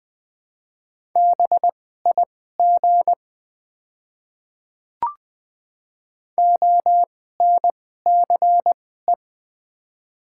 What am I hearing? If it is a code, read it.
Morse code, heard as a keyed tone, about 20 wpm, spaced as for 14 wpm, 700 Hz. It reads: BIG ONCE
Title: 2x Character Spacing - Top 300 Words - 20wpm